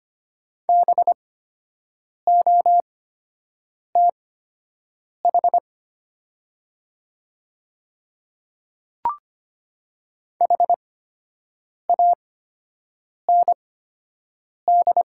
Morse code, heard as a keyed tone, about 25 wpm, spaced as for 7 wpm, 700 Hz. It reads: BOTH HAND
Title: 8x Character Spacing - Top 300 Words - 25wpm